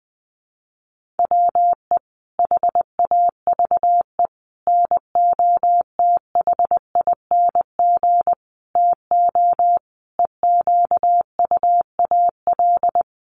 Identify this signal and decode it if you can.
Morse code, heard as a keyed tone, about 20 wpm, 700 Hz. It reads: WE HAVE NOTHING TO EQUAL
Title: Sentences from Top 1400 Words 20wpm